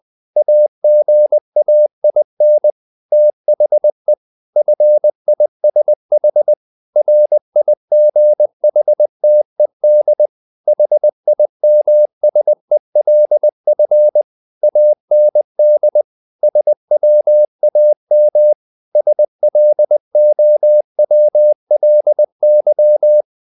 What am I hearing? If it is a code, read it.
Morse code, heard as a keyed tone, about 20 wpm, 600 Hz. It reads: AGAIN THE FISH RIGHTED HIMSELF AND SWAM SLOWLY
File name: the old man and the sea part 2 20wpm